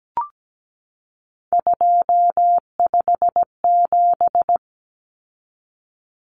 Morse code, heard as a keyed tone, about 17 wpm, 700 Hz. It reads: 257
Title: Common Cut Numbers - 3-Digit 17wpm